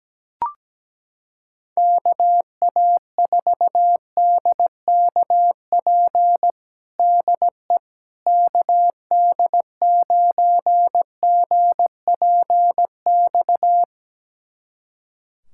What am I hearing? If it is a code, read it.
Morse code, heard as a keyed tone, about 17 wpm, 700 Hz. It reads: KA4DKP DE KD9GPX
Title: USA Callsign DE Callsign 17wpm